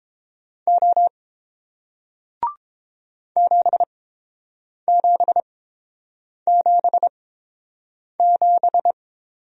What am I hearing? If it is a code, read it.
Morse code, heard as a keyed tone, about 31 wpm, spaced as for 7 wpm, 700 Hz. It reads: O 7777